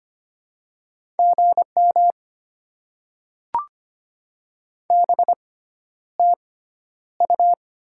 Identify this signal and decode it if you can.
Morse code, heard as a keyed tone, about 25 wpm, spaced as for 11 wpm, 690 Hz. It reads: GM BTU
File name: Top 20 QSO Elements - 6x Character Spacing - 25wpm